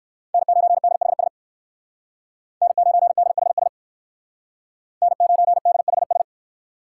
Morse code, heard as a keyed tone, about 59 wpm, 700 Hz. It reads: N9DHS N9DHS N9DHS